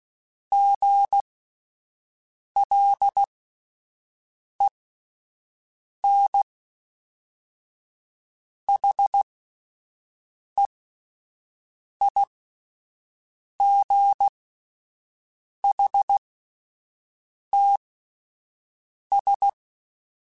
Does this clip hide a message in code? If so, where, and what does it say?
Yes: Morse code from 0.5 to 19.5 s, heard as a keyed tone, about 16 wpm, spaced as for 5 wpm, 800 Hz. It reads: GLENHEIGHTS